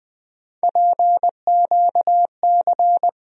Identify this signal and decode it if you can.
Morse code, heard as a keyed tone, about 20 wpm, 700 Hz. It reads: PQC
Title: USA Call Sign Postfixes - 3 Letters 20wpm